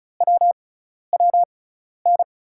Morse code, heard as a keyed tone, about 34 wpm, spaced as for 12 wpm, 700 Hz. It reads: WWN
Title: USA Call Signs - 5 Characters - 6x Character Spacing - 35wpm